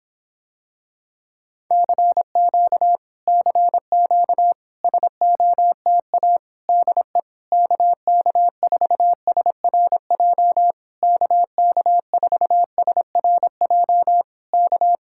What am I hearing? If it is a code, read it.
Morse code, heard as a keyed tone, about 26 wpm, 700 Hz. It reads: CQ CQ SOTA DE KK4SRJ KK4SRJ K